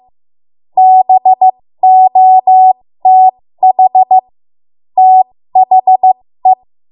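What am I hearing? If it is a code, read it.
Morse code, heard as a keyed tone, about 15 wpm, spaced as for 13 wpm, 750 Hz. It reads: BOTH THE
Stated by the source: 13 wpm Morse code practice archive transmission.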